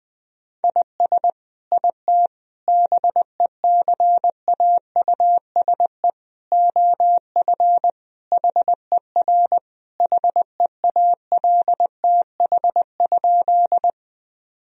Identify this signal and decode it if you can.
Morse code, heard as a keyed tone, about 21 wpm, 700 Hz. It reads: IS IT BECAUSE OF HER HEALTH?
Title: Sentences from Top 1000 Words - Review Part 3 - 20wpm